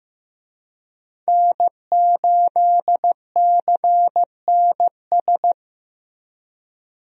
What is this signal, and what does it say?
Morse code, heard as a keyed tone, about 15 wpm, 700 Hz. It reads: N8CNS
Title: USA Call Signs - 5 Characters - Repeated 3x - 15wpm